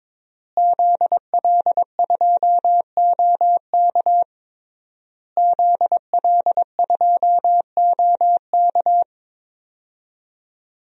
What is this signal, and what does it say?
Morse code, heard as a keyed tone, about 22 wpm, 700 Hz. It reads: ZL2OK ZL2OK
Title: DXCC-100 Call Signs - 5 Characters - Repeated 3x - 22wpm